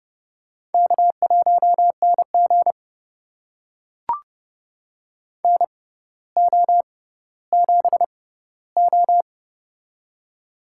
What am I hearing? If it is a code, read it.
Morse code, heard as a keyed tone, about 30 wpm, spaced as for 15 wpm, 700 Hz. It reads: K1NG NO7O
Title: USA Call Signs - 4 Characters - 6x Character Spacing - 30wpm